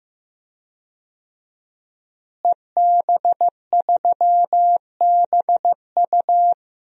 Morse code, heard as a keyed tone, about 15 wpm, 700 Hz. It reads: EB3BU
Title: DXCC-100 Call Signs - 5 Characters - Repeated 2x - 15wpm